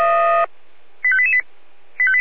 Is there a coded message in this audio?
Yes: five-tone selective call (ZVEI) from 1.0 to 1.4 s, heard as quick runs of five tones, tones of about 70 ms. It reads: 85888